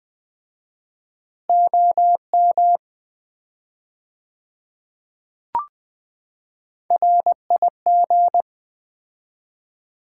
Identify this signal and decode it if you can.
Morse code, heard as a keyed tone, about 20 wpm, 700 Hz. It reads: OM RIG
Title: Common QSO Phrases - 4x Word-Spacing - 20wpm